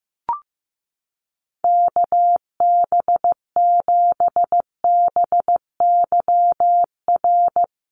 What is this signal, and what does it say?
Morse code, heard as a keyed tone, about 15 wpm, 700 Hz. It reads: KB7BYR
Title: USA Call Signs Repeated 2x 15wpm